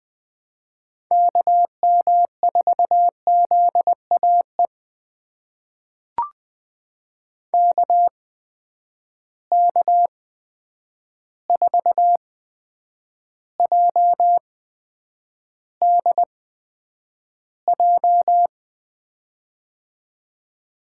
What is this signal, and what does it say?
Morse code, heard as a keyed tone, about 20 wpm, spaced as for 9 wpm, 700 Hz. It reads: KM4ZAE KK4JDJ
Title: USA Call Signs - 8x Character Spacing - 20wpm